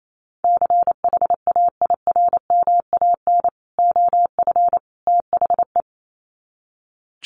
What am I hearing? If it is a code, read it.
Morse code, heard as a keyed tone, about 28 wpm, 700 Hz. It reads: CHAIRMAN OF THE
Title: Sets of 3 Words 28wpm